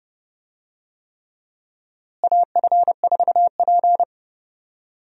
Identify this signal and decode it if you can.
Morse code, heard as a keyed tone, about 32 wpm, 700 Hz. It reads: AF4P
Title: USA Call Signs - 4 Characters - Repeated 3x - 30wpm